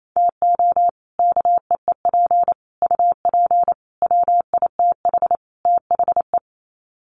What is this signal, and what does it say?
Morse code, heard as a keyed tone, about 28 wpm, 700 Hz. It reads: TO KEEP UP WITH THE
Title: Sets of 5 Words Encore 28wpm